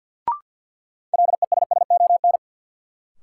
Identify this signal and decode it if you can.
Morse code, heard as a keyed tone, about 53 wpm, 700 Hz. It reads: LESSON
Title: Sentences from Top 1200 Words - New Word Drill - 50wpm